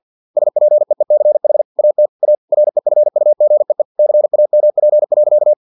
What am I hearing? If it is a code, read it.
Morse code, heard as a keyed tone, about 50 wpm, 600 Hz. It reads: SPEECH AT A REFUGEE CAMP.
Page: Never 49wpm